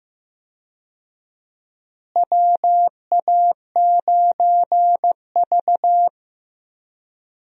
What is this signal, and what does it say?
Morse code, heard as a keyed tone, about 15 wpm, 700 Hz. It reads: WA9V
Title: USA Call Signs - 4 Characters - Repeated 2x - 15wpm